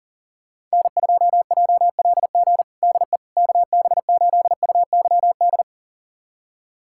Morse code, heard as a keyed tone, about 40 wpm, 700 Hz. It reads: N2JLG DE KB8UYD
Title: USA Callsign DE Callsign 40wpm